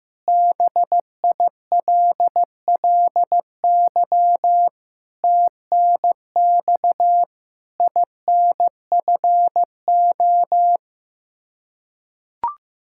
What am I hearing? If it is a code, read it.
Morse code, heard as a keyed tone, about 15 wpm, 700 Hz. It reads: BILLY TNX INFO
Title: K1USN SST QSO Practice 15wpm